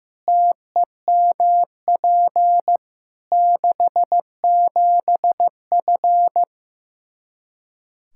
Morse code, heard as a keyed tone, about 15 wpm, 700 Hz. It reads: TEMP 67F
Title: K1USN SST QSO Practice 15wpm